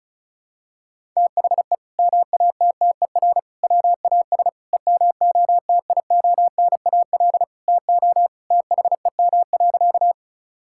Morse code, heard as a keyed tone, about 36 wpm, 700 Hz. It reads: THE MATTER WAS EMOTIONAL TO THEM.
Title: Sentences from Top 1000 Words - Review Part 5 - 35wpm